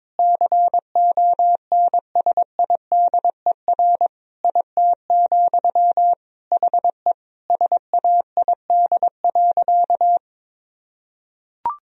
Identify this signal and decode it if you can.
Morse code, heard as a keyed tone, about 22 wpm, 700 Hz. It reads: CONSIDER IT, HE SAID.
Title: Sentences from Top 1000 Words - Review Part 4 - 22wpm